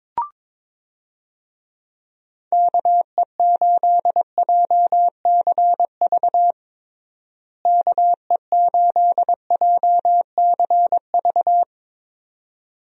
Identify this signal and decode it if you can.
Morse code, heard as a keyed tone, about 22 wpm, 700 Hz. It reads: KE8JCV KE8JCV
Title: USA Call Signs - 6 Characters - Repeated 2x - 22wpm